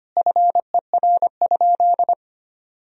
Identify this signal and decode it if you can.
Morse code, heard as a keyed tone, about 25 wpm, 700 Hz. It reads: FER?
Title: Sentences from Top 100 Words and Abbreviations 25wpm